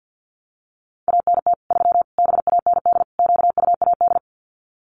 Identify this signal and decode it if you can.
Morse code, heard as a keyed tone, about 50 wpm, 700 Hz. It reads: ANT 3 BAND QUAD
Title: Common QSO Phrases 50wpm